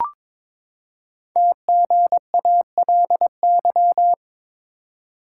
Morse code, heard as a keyed tone, about 22 wpm, 700 Hz. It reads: TGALY
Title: Code Groups - 5 Letters - 22wpm